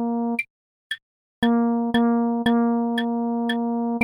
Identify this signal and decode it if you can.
There is no code.